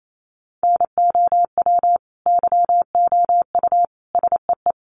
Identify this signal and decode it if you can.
Morse code, heard as a keyed tone, about 28 wpm, 700 Hz. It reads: NOW YOU SEE
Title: Sentences from Top 1500 Words 28wpm